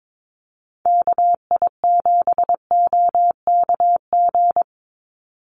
Morse code, heard as a keyed tone, about 22 wpm, 700 Hz. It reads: KI7OKG
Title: USA Call Signs 22wpm